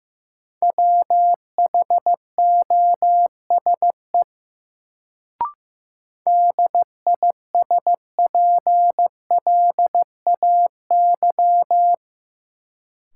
Morse code, heard as a keyed tone, about 15 wpm, 700 Hz. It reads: WHOSE DISPLAY
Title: Sentences from Top 1200 Words - New Word Drill Alt - 15wpm